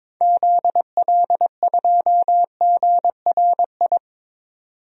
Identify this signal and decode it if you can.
Morse code, heard as a keyed tone, about 22 wpm, 700 Hz. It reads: ZL2GRI
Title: DXCC-100 Call Signs - 6 Characters - 22wpm